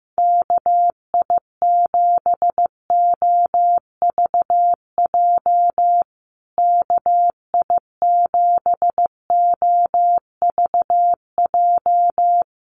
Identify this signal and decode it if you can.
Morse code, heard as a keyed tone, about 15 wpm, 700 Hz. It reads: KI7OVJ KI7OVJ
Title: USA Call Signs Repeated 3x 15wpm